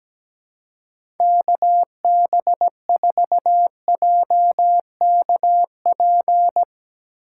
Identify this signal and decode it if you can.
Morse code, heard as a keyed tone, about 17 wpm, 700 Hz. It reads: KB4JKP
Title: USA Call Signs - 6 Characters - 3x Character Spacing - 17wpm